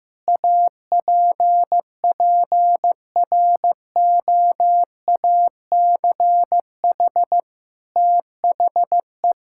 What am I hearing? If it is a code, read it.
Morse code, heard as a keyed tone, about 15 wpm, 700 Hz. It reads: APPROACH THE
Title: Sentences from Top 1000 Words - Review Part 6 - 15wpm